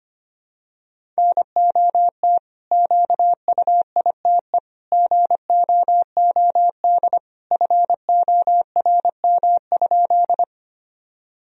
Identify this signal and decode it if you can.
Morse code, heard as a keyed tone, about 25 wpm, 700 Hz. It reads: NOT QUITE GOOD FORM?
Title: Sentences from Top 1000 Words - Review Part 1 - 25wpm